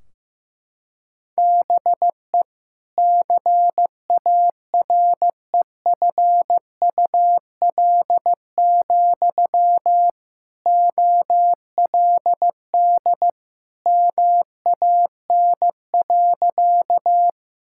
Morse code, heard as a keyed tone, about 14 wpm, 700 Hz. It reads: BE CAREFUL, OLD MAN.
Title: Sentences from Top 1600 Words 15wpm